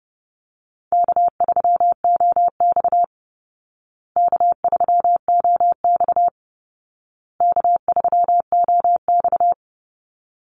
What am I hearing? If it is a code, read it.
Morse code, heard as a keyed tone, about 30 wpm, 700 Hz. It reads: K3OX K3OX K3OX
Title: Easy Call Signs Repeated 3x - 30wpm